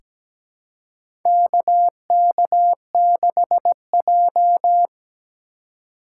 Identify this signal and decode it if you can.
Morse code, heard as a keyed tone, about 17 wpm, 700 Hz. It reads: KK6J
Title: USA Call Signs - 4 Characters - 2x Character Spacing - 17wpm